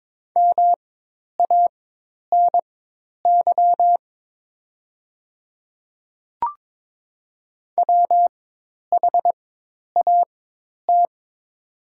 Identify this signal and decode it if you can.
Morse code, heard as a keyed tone, about 22 wpm, spaced as for 10 wpm, 700 Hz. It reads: MANY WHAT
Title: 4x Character Spacing - Top 100 Words - 22wpm